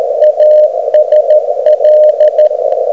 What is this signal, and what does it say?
Morse code, heard as a keyed tone, about 13 wpm, 610 Hz, with radio noise. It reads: ASL
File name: Beacon recordings / OK0EF